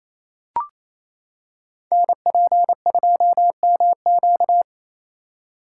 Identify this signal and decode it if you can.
Morse code, heard as a keyed tone, about 29 wpm, 700 Hz. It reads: NP2MQ
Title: Code Groups - 5 Letter:Numbers - 28wpm